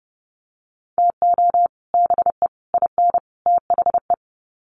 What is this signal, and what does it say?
Morse code, heard as a keyed tone, about 30 wpm, 700 Hz. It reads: TO BE IN THE
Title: Sets of 4 Words 30wpm